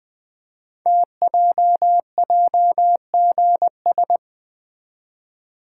Morse code, heard as a keyed tone, about 20 wpm, 700 Hz. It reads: TJJGS
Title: Code Groups - 5 Letters - 20wpm